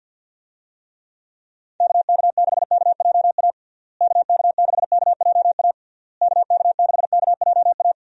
Morse code, heard as a keyed tone, about 50 wpm, 700 Hz. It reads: KK6KJA KK6KJA KK6KJA
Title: USA Call Signs - 6 Characters - Repeated 3x - 50wpm